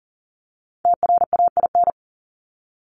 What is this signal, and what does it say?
Morse code, heard as a keyed tone, about 40 wpm, 700 Hz. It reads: TRAIN
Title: Top 1751-2000 Words 40wpm